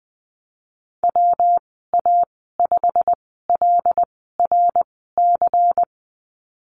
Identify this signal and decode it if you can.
Morse code, heard as a keyed tone, about 20 wpm, spaced as for 15 wpm, 700 Hz. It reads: WA5LRC
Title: 179 Lesson 29g - Slash - Call Signs 20wpm